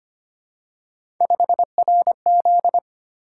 Morse code, heard as a keyed tone, about 25 wpm, 700 Hz. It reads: 5RZ
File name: Three Letter-Number Combinations - 25wpm